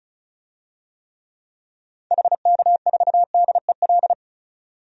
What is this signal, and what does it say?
Morse code, heard as a keyed tone, about 36 wpm, 700 Hz. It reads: HK4DEL
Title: DXCC-100 Call Signs - 6 Characters - 35wpm